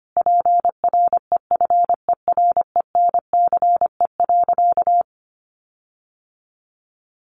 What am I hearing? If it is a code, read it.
Morse code, heard as a keyed tone, about 25 wpm, 700 Hz. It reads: PREFERENCE.
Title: Sentences from Top 1500 Words 25wpm